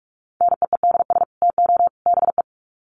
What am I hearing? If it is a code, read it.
Morse code, heard as a keyed tone, about 44 wpm, 700 Hz. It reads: NEEDS TO BE
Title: Sets of 3 Words 45wpm